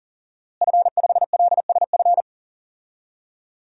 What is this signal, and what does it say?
Morse code, heard as a keyed tone, about 40 wpm, 700 Hz. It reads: F5LSF
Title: DXCC-100 Call Signs - 5 Characters - Repeated 3x - 40wpm